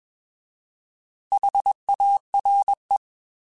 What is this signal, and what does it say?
Morse code, heard as a keyed tone, about 21 wpm, 800 Hz. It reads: HARE
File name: Alphabetical / Alphabetical mp3s / Hs @20